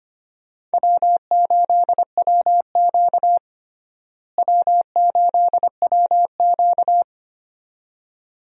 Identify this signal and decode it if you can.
Morse code, heard as a keyed tone, about 25 wpm, 700 Hz. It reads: W8WQ W8WQ
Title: USA Call Signs - 4 Characters - Repeated 2x - 25wpm